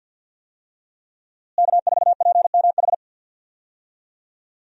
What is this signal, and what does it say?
Morse code, heard as a keyed tone, about 50 wpm, 700 Hz. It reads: K4PMH
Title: USA Call Signs - 5 Characters - Repeated 3x - 50wpm